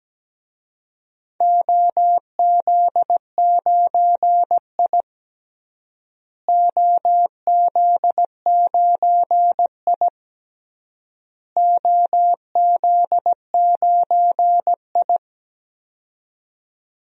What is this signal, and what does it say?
Morse code, heard as a keyed tone, about 17 wpm, 700 Hz. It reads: OZ9I OZ9I OZ9I
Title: DXCC-100 Call Signs - 4 Characters - Repeated 3x - 17wpm